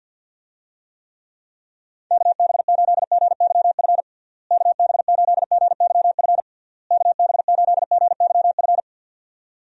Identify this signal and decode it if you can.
Morse code, heard as a keyed tone, about 52 wpm, 700 Hz. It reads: KB8GYF KB8GYF KB8GYF
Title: USA Call Signs - 6 Characters - Repeated 3x - 50wpm